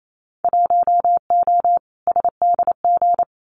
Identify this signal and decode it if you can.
Morse code, heard as a keyed tone, about 28 wpm, 700 Hz. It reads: 1O SDG
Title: Winter Field Day Exchanges Encore 28wpm